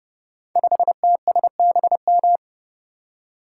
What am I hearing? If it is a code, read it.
Morse code, heard as a keyed tone, about 30 wpm, 700 Hz. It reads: 5TSBM
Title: Code Groups - 5 Letter:Numbers - 30wpm